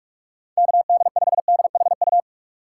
Morse code, heard as a keyed tone, about 46 wpm, 700 Hz. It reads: KD5DHU